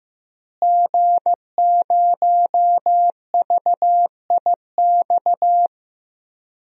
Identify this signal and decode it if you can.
Morse code, heard as a keyed tone, about 15 wpm, 700 Hz. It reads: G0VIX
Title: DXCC-100 Call Signs - 5 Characters - Repeated 3x - 15wpm